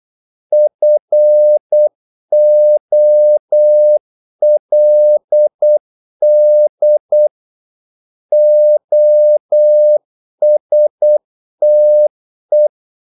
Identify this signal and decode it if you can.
Morse code, heard as a keyed tone, about 8 wpm, 600 Hz. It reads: FOLD OSTE
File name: Korte ord DK 8wpm